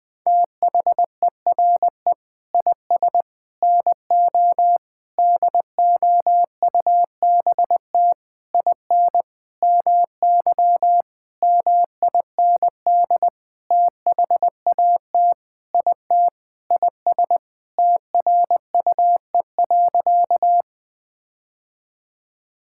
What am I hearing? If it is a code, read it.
Morse code, heard as a keyed tone, about 20 wpm, 700 Hz. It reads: THERE IS NO DOUBT IN MY MIND THAT IT IS TRUE.
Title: Sentences with 6-Word Phrases - 20wpm